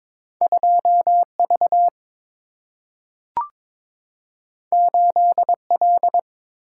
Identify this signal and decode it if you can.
Morse code, heard as a keyed tone, about 22 wpm, 700 Hz. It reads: 2V 8L
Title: Single-Letter Single-Number 22wpm